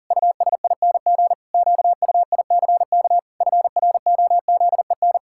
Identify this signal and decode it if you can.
Morse code, heard as a keyed tone, about 40 wpm, 700 Hz. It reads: USING QUICK FROZEN
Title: Pangram Challenge v2 - 40wpm